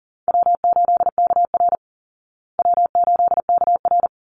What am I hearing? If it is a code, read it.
Morse code, heard as a keyed tone, about 40 wpm, 700 Hz. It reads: W8KR W8KR
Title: Call Signs Repeated 3x - 40wpm